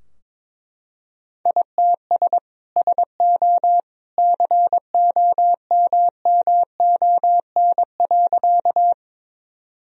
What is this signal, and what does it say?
Morse code, heard as a keyed tone, about 22 wpm, 700 Hz. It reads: ITS SO COMMON.
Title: Sentences from Top 1000 Words - Review Part 3 - 22wpm